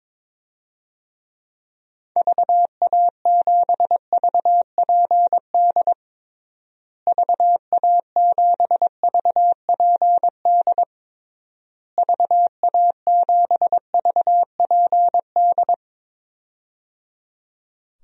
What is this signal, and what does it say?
Morse code, heard as a keyed tone, about 22 wpm, 700 Hz. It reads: VA7VPD VA7VPD VA7VPD
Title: DXCC-100 Call Signs - 6 Characters - Repeated 3x - 22wpm